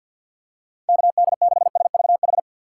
Morse code, heard as a keyed tone, about 51 wpm, 700 Hz. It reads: KD6SVH
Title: USA Call Signs - 3x Character Spacing - 50wpm